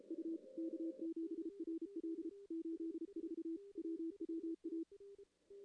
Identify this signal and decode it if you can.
Morse code, heard as a keyed tone, about 32 wpm, 330 Hz. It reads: U K7RL 84 WWA
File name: contests / CW-SS-k5tr- -1600